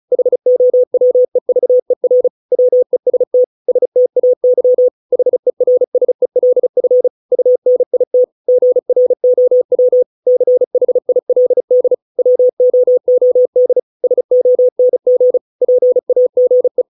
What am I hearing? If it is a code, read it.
Morse code, heard as a keyed tone, about 34 wpm, 500 Hz. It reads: HOWEVER WEST STAY HERSELF UNIT GROW CHILD WOOD SONG PAGE